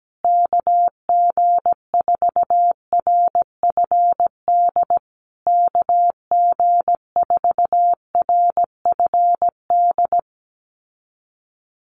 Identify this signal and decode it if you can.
Morse code, heard as a keyed tone, about 17 wpm, 700 Hz. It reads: KG4RFD KG4RFD
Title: USA Call Signs Repeated 3x 17wpm